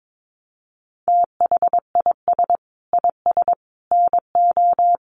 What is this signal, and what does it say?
Morse code, heard as a keyed tone, about 22 wpm, 700 Hz. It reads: THIS IS NO
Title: Sentences from Top 700 Words 22wpm